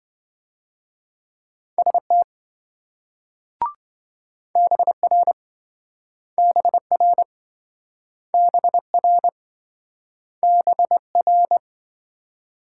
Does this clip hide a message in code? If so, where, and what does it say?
Yes: Morse code from 1.8 to 11.6 s, heard as a keyed tone, about 26 wpm, 700 Hz. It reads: ST BR BR BR BR